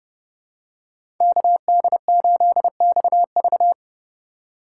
Morse code, heard as a keyed tone, about 32 wpm, 700 Hz. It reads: KD8XV
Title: USA Call Signs - 2x Character Spacing - 30wpm